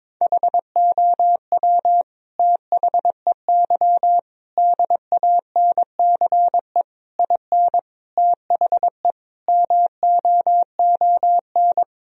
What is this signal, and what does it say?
Morse code, heard as a keyed tone, about 22 wpm, 700 Hz. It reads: HOW THEY DANCE IN THE MOON
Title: Sentences from Top 1000 Words - Review Part 4 - 22wpm